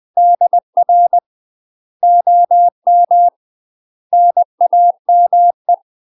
Morse code, heard as a keyed tone, about 20 wpm, 700 Hz. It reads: DR OM NAME
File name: UK MASTER QSOs TRAINING / 20WPM / UK MASTER QSOs 0002